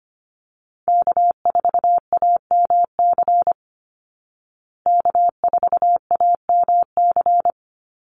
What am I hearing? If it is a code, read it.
Morse code, heard as a keyed tone, about 25 wpm, 700 Hz. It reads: K4AMC K4AMC
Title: Call Signs Repeated 2x - 25wpm